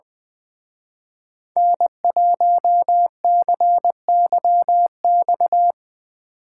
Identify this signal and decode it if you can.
Morse code, heard as a keyed tone, about 24 wpm, 700 Hz. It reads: N1CYX
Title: Speed Racing - USA 1x3 Call Signs 20wpm